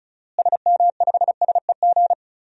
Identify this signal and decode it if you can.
Morse code, heard as a keyed tone, about 34 wpm, 700 Hz. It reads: SM5SEG
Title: DXCC-100 Call Signs - 6 Characters - Repeated 3x - 35wpm